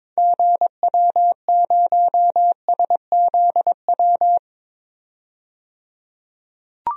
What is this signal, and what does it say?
Morse code, heard as a keyed tone, about 22 wpm, 700 Hz. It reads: GW0SZW
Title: DXCC-100 Call Signs - 6 Characters - 22wpm